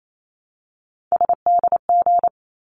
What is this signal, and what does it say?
Morse code, heard as a keyed tone, about 28 wpm, 700 Hz. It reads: SDG
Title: ARRL Sections 28wpm